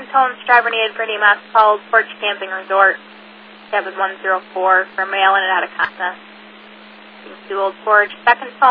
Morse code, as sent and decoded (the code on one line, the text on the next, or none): none